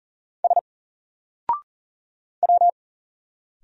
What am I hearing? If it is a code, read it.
Morse code, heard as a keyed tone, about 39 wpm, 700 Hz. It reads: S W
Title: Single Character (Letter:Number:Punctuation:Prosign) Alt 40wpm